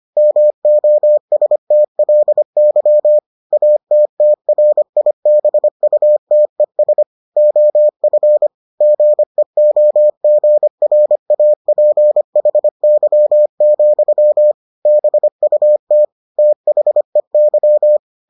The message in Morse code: MOSTLY ATTRIBUTES OF GEOGRAPHY, BUT THEY

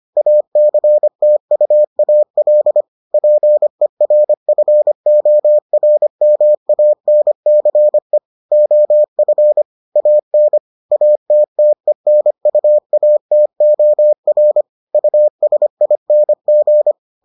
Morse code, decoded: ACTUAL PERFORMANCE OF AN ATTENUATOR USING

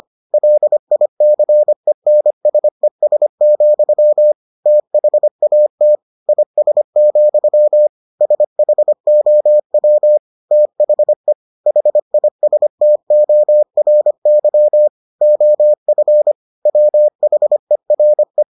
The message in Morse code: LICENSES, THAT IS, SHOW THE HISTORY OF WHERE

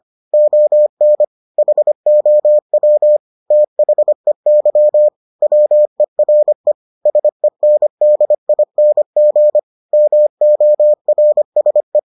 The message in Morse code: ON HOW THEY WERE SENDING MORSE

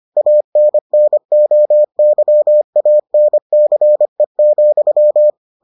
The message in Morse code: ANNOYANCE,

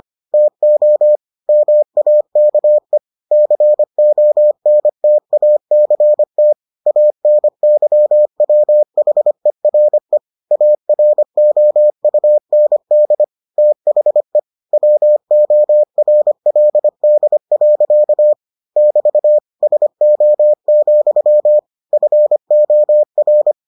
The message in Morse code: TO MAKE CONTACT ANYWHERE AROUND THE WORLD. <BT> SO, FOR

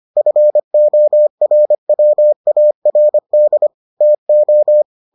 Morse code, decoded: FORWARD TO

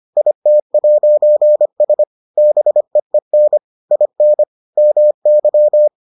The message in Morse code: IT'S BEEN IN MY

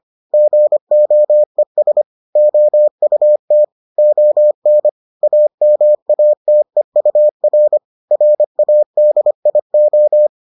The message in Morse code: GOES OUT ON AMATEUR RADIO

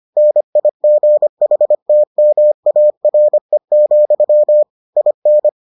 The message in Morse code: NIGHTMARE, IN

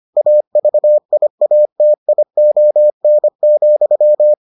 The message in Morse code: AVIATION,

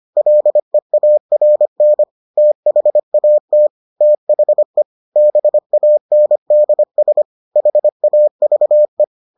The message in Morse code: LEARN THAT THE BANDS HAVE

